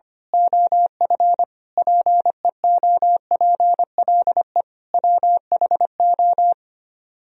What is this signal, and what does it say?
Morse code, heard as a keyed tone, about 24 wpm, 700 Hz. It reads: OF PEOPLE WHO
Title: Speed-Racing - Sets of 5 Words 25wpm